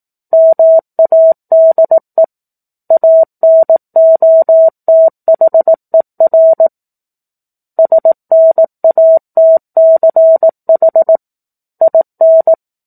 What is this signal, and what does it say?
Morse code, heard as a keyed tone, about 18 wpm, 650 Hz. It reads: MADE ANOTHER SNATCH IN
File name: 04-Chapter-04 18 WPM